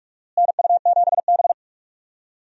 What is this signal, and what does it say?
Morse code, heard as a keyed tone, about 44 wpm, 700 Hz. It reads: NU7B
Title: USA Call Signs - 4 Characters - Repeated 3x - 45wpm